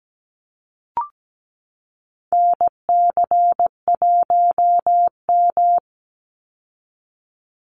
Morse code, heard as a keyed tone, about 17 wpm, 700 Hz. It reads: NC1M
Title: Call Signs Repeated 2x - 17wpm